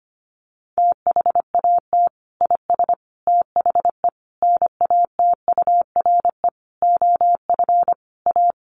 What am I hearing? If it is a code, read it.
Morse code, heard as a keyed tone, about 25 wpm, 700 Hz. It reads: THAT IS THE NATURE OF A
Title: Sentences from Top 1400 Words 25wpm